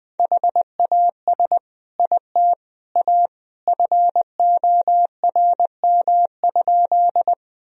Morse code, heard as a keyed tone, about 20 wpm, 700 Hz. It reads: HAS IT A FORM?
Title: Sentences from Top 1000 Words - Review Part 1 - 20wpm